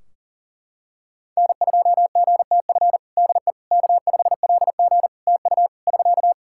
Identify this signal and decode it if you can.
Morse code, heard as a keyed tone, about 40 wpm, 700 Hz. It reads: N2GTF DE K5LG TU <SK>
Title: Practice QSO - Exchange 4 - Ending - 40wpm